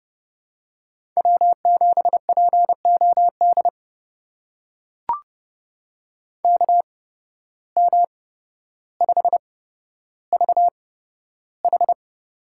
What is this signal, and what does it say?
Morse code, heard as a keyed tone, about 30 wpm, spaced as for 13 wpm, 700 Hz. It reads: W7POD KM5VH
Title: USA Call Signs - 5 Characters - 8x Character Spacing - 30wpm